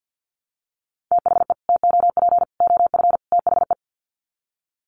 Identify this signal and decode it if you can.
Morse code, heard as a keyed tone, about 50 wpm, 700 Hz. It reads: THE TOP OF THE
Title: Sets of 4 Words 50wpm